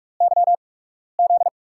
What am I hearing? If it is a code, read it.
Morse code, heard as a keyed tone, about 44 wpm, spaced as for 12 wpm, 700 Hz. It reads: YZ
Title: USA Call Signs - 8x Character Spacing - 45wpm